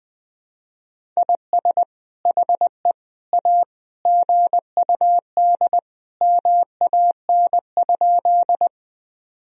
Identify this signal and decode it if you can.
Morse code, heard as a keyed tone, about 20 wpm, 700 Hz. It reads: IS HE A GUD MAN?
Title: Sentences from Top 200 Words and Abbreviations 20wpm